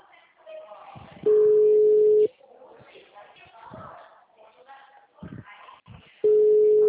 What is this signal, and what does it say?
Ringback tone (Europe, 425 Hz), with speech.
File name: out- -2004- - - .709